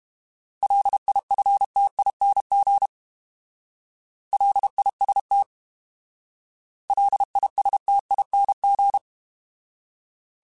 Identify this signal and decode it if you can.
Morse code, heard as a keyed tone, about 32 wpm, 800 Hz. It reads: LIFTING LIST LISTING
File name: Alphabetical / Alphabetical mp3s / Ls @30